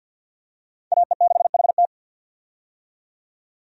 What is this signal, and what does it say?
Morse code, heard as a keyed tone, about 52 wpm, 700 Hz. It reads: AE6HT
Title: USA Call Signs - 5 Characters - 50wpm